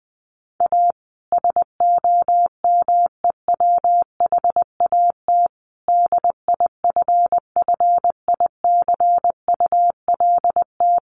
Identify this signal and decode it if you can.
Morse code, heard as a keyed tone, about 20 wpm, 700 Hz. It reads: A SOMEWHAT DIFFICULT
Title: Sentences from Top 1300 Words 20wpm